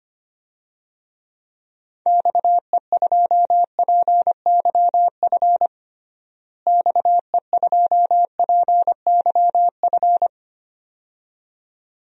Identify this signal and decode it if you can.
Morse code, heard as a keyed tone, about 25 wpm, 700 Hz. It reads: XE2PYF XE2PYF
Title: DXCC-100 Call Signs - 6 Characters - Repeated 2x - 25wpm